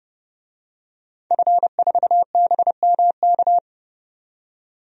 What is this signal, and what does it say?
Morse code, heard as a keyed tone, about 32 wpm, 700 Hz. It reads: F4BMK
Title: DXCC-100 Call Signs - 5 Characters - 30wpm